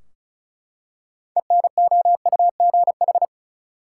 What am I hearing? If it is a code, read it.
Morse code, heard as a keyed tone, about 34 wpm, 700 Hz. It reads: ENOUGH
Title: Six Letter Words - 35wpm